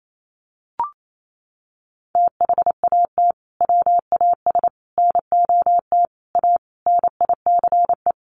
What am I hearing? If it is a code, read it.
Morse code, heard as a keyed tone, about 28 wpm, 700 Hz. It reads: THAT WAS NOT A NICE
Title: Sentences from Top 1200 Words 28wpm